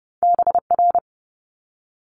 Morse code, heard as a keyed tone, about 30 wpm, 700 Hz. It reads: BR
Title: Multi-letter Phonemes Quick Fire 30wpm